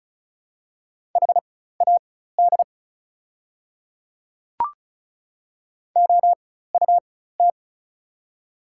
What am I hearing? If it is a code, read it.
Morse code, heard as a keyed tone, about 34 wpm, spaced as for 16 wpm, 700 Hz. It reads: HAD OUT
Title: 4x Character Spacing - Top 100 Words - 35wpm